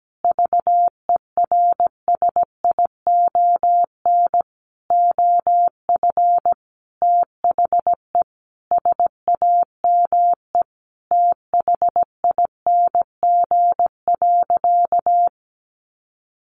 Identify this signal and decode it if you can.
Morse code, heard as a keyed tone, about 17 wpm, 700 Hz. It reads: VERSION OF THE SAME THING.
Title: Sentences from Top 1400 Words 17wpm